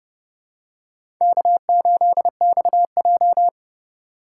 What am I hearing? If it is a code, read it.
Morse code, heard as a keyed tone, about 30 wpm, 700 Hz. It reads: K8XJ
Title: K1USN SST QSO Practice - 30wpm